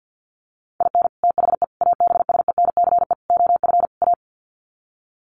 Morse code, heard as a keyed tone, about 50 wpm, 700 Hz. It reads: IN THE ABSENCE OF A
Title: Sets of 5 Words Encore 50wpm